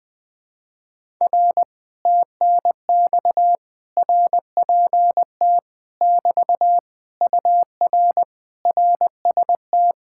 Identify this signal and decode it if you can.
Morse code, heard as a keyed tone, about 20 wpm, 700 Hz. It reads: R TNX RPT <BT> UR RST
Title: Speed-Racing Common QSO Phrases Encore - 17wpm